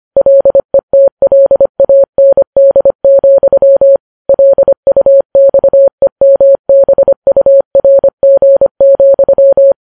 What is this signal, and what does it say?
Morse code, heard as a keyed tone, about 25 wpm, 550 Hz. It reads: LETLAND, LUXEMBURG,
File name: Files Web tls / Morse on9fv / morse25kps